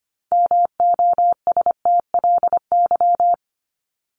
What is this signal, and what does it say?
Morse code, heard as a keyed tone, about 25 wpm, 700 Hz. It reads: MOSTLY
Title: Top 1001-1250 Words 25wpm